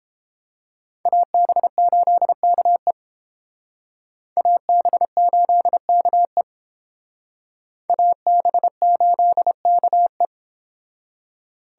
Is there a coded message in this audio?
Yes: Morse code from 1.1 to 10.3 s, heard as a keyed tone, about 30 wpm, 700 Hz. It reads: AB8KE AB8KE AB8KE